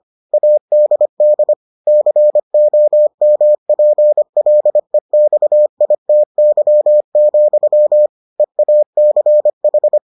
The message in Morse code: ADD COMPLEXITY, EACH

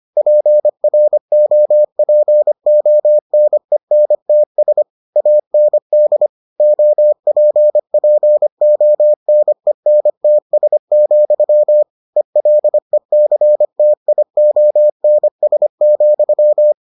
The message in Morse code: PROPONENTS AND OPPONENTS, ELECTIONS,

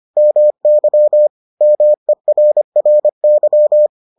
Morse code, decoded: MY MERRY

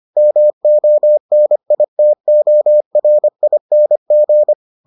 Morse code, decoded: MONITORING

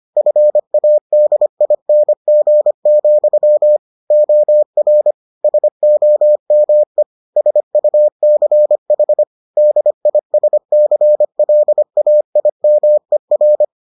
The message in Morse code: FADING, OR SOME SUCH DISCLAIMER

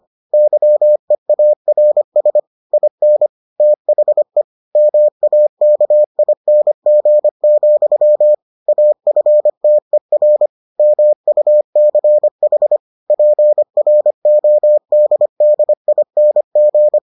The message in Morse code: YEARS IN THE MAKING, AFTER MUCH PRODDING

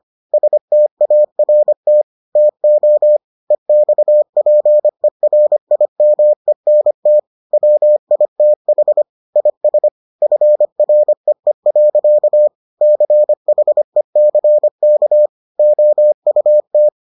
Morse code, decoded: START TO EXPERIMENT WITH IS FREE. CHECK OUT

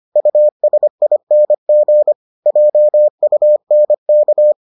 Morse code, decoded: USING JUNK

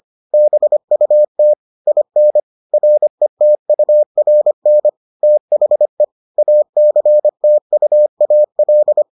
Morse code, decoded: BUT IN RETURN THE ACTUAL

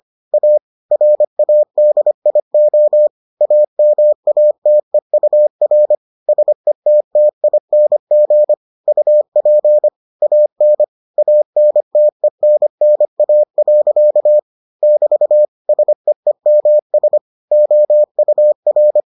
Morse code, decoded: A RADIO AMATEUR SETTING UP AN ANTENNA. <BT> SEEMS OUR

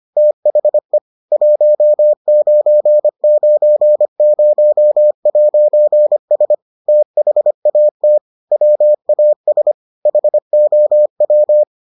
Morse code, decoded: THE 1990'S THAT WAS HOW

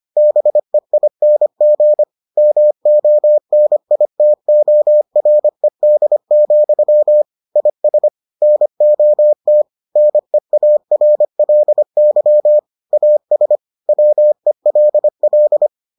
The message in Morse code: BEING MONITORED, IS NOT NEARLY AS WELL

